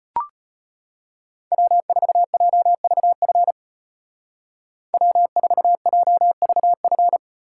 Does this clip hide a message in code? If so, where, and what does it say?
Yes: Morse code from 1.5 to 7.2 s, heard as a keyed tone, about 32 wpm, 700 Hz. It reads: W4JVF W4JVF